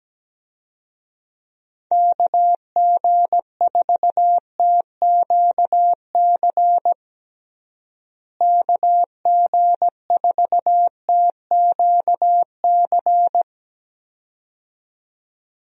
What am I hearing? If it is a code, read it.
Morse code, heard as a keyed tone, about 17 wpm, 700 Hz. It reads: KG4TQC KG4TQC
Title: USA Call Signs - 6 Characters - Repeated 2x - 17wpm